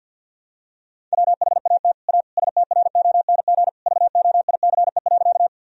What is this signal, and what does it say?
Morse code, heard as a keyed tone, about 50 wpm, 700 Hz. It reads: WHAT A STRONG VOICE.
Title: Sentences from Top 1000 Words - Review Part 3 - 50wpm